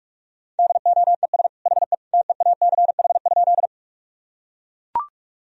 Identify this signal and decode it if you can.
Morse code, heard as a keyed tone, about 41 wpm, 700 Hz. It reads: DOES HE TEACH?
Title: Sentences from Top 1000 Words - Review Part 2 - 45wpm